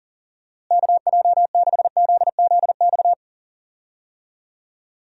Morse code, heard as a keyed tone, about 39 wpm, 700 Hz. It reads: KJ6ZZX
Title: USA Call Signs - 6 Characters - Repeated 2x - 40wpm